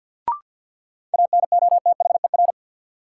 Morse code, heard as a keyed tone, about 53 wpm, 700 Hz. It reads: ANOTHER
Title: Sentences from Top 600 Words - New Word Drill - 50wpm